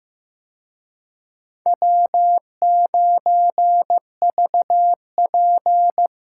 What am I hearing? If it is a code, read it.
Morse code, heard as a keyed tone, about 18 wpm, 700 Hz. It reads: W9VP
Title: Speed Racing - USA 1x2 Call Signs 15wpm